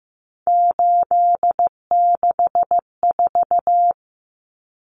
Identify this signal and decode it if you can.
Morse code, heard as a keyed tone, about 15 wpm, 700 Hz. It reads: 864
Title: Common Cut Numbers - 3-Digit 15wpm